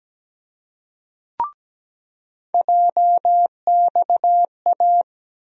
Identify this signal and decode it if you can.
Morse code, heard as a keyed tone, about 17 wpm, 700 Hz. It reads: JXA
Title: USA Call Sign Postfixes - 3 Letters 17wpm